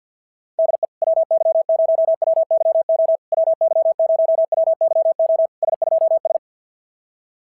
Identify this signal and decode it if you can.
Morse code, heard as a keyed tone, about 46 wpm, 650 Hz. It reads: DE WY0WYO WY0WYO S2S
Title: SOTA QSO Practice 50wpm